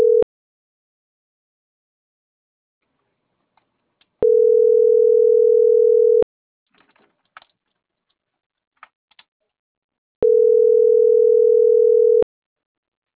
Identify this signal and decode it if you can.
Ringback tone (North America), 440 + 480 Hz.
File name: CENTRAL MULTIMODAL / out francisco.cobos